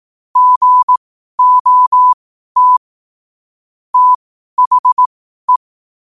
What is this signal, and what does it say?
Morse code, heard as a keyed tone, about 18 wpm, spaced as for 12 wpm, 1000 Hz. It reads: GOT THE
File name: qotdfiles / 12wpm